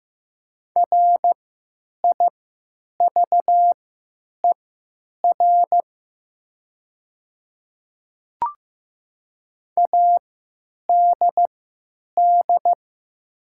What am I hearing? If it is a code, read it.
Morse code, heard as a keyed tone, about 15 wpm, spaced as for 9 wpm, 700 Hz. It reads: RIVER ADD
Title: 3x Character Spacing - Top 300 Words - 15wpm